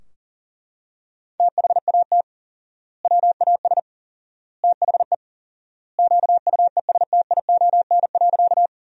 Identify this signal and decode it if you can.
Morse code, heard as a keyed tone, about 40 wpm, 700 Hz. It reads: THAT WAS THE QUESTION.
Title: 4x Word-Spacing - Sentences from Top 400 Words - 40wpm